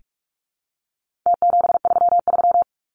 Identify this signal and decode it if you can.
Morse code, heard as a keyed tone, about 44 wpm, 700 Hz. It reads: T733
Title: Numbers - Contest Style - 4-Digit 45wpm